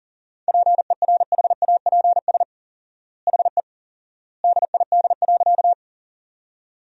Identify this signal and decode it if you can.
Morse code, heard as a keyed tone, about 40 wpm, 700 Hz. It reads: PERHAPS HE DID.
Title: 4x Word-Spacing - Sentences from Top 500 Words - 40wpm